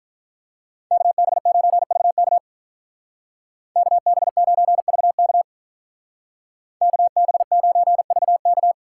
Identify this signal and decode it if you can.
Morse code, heard as a keyed tone, about 44 wpm, 700 Hz. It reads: KB9VK KB9VK KB9VK